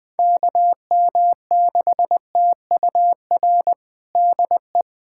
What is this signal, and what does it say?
Morse code, heard as a keyed tone, about 20 wpm, 700 Hz. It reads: KM6TUR DE
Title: USA Callsign DE Callsign 20wpm